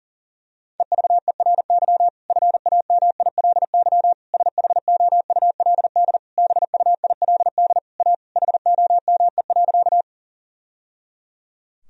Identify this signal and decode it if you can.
Morse code, heard as a keyed tone, about 40 wpm, 700 Hz. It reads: EVERY FAMILY SHOULD BUILD A HOME.
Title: Sentences from Top 1000 Words - Review Part 1 - 40wpm